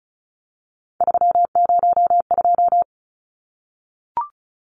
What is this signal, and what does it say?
Morse code, heard as a keyed tone, about 34 wpm, 700 Hz. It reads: 302
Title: Numbers - 3-Digit 35wpm